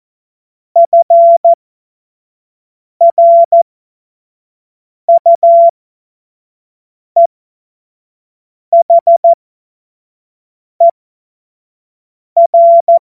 Morse code, heard as a keyed tone, about 14 wpm, spaced as for 5 wpm, 680 Hz. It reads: FRUEHER
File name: Morsekurs / Witze / Witz 18. April 2023 25 F70